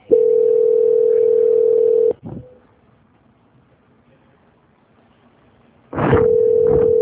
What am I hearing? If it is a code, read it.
Ringback tone (North America), 440 + 480 Hz.
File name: out- -2035- - - .138